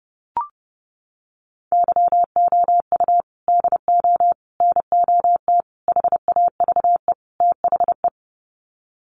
Morse code, heard as a keyed tone, about 30 wpm, 700 Hz. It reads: YOU DO NOT HAVE THE
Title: Sets of 5 Words 30wpm